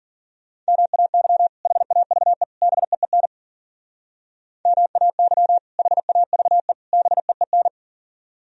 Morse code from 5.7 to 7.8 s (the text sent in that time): HAVE BEEN